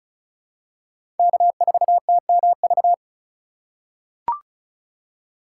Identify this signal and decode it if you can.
Morse code, heard as a keyed tone, about 34 wpm, 700 Hz. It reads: K4TMV
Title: USA Call Signs - 5 Characters - 2x Character Spacing - 35wpm